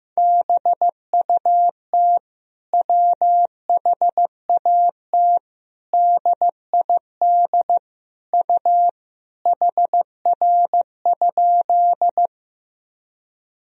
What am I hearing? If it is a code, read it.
Morse code, heard as a keyed tone, about 16 wpm, 700 Hz. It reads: BUT WHAT DID U HR?
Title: Sentences from Top 300 Words and Abbreviations 15wpm